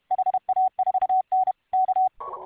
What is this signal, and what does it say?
Morse code, heard as a keyed tone, about 31 wpm, 750 Hz. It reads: HA4N K